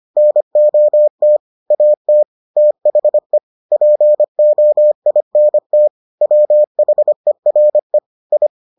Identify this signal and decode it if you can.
Morse code, heard as a keyed tone, about 25 wpm, 600 Hz. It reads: NOT AT THE POINT WHERE I